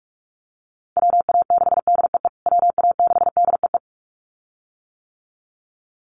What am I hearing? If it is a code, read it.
Morse code, heard as a keyed tone, about 44 wpm, 700 Hz. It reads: WA6DEE WA6DEE
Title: USA Call Signs Repeated 2x 45wpm